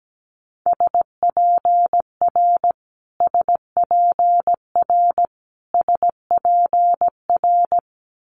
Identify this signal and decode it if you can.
Morse code, heard as a keyed tone, about 17 wpm, 700 Hz. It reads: SPR SPR SPR
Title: Multi-letter Phonemes Repeated 3x 17wpm